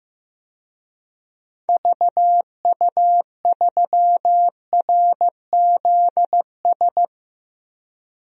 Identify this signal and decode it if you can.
Morse code, heard as a keyed tone, about 15 wpm, 700 Hz. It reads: VU3RZS
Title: DXCC-100 Call Signs - 6 Characters - 15wpm